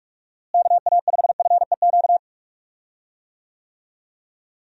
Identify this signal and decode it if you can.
Morse code, heard as a keyed tone, about 46 wpm, 700 Hz. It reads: KA5FEQ
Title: USA Call Signs - 6 Characters - Repeated 3x - 45wpm